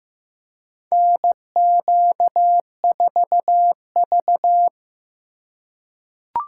Morse code, heard as a keyed tone, about 15 wpm, 700 Hz. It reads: NQ4V
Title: USA Call Signs - 4 Characters - 4x Character Spacing - 15wpm